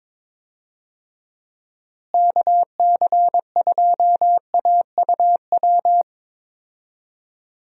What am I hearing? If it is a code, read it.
Morse code, heard as a keyed tone, about 21 wpm, 700 Hz. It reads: KC2AUW
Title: USA Call Signs - 6 Characters - Repeated 2x - 22wpm